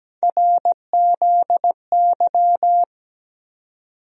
Morse code, heard as a keyed tone, about 17 wpm, 700 Hz. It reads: RZY
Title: USA Call Sign Postfixes - 3 Letters 17wpm